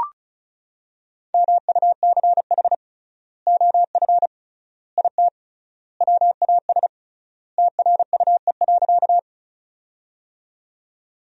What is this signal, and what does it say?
Morse code, heard as a keyed tone, about 36 wpm, 700 Hz. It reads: MUCH OF IT WAS TRUE.
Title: 3x Word-Spacing - Sentences from Top 400 Words - 35wpm